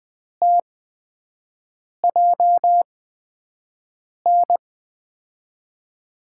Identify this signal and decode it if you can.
Morse code, heard as a keyed tone, about 21 wpm, spaced as for 5 wpm, 700 Hz. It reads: TJN
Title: USA Call Signs - 5 Characters - 8x Character Spacing - 20wpm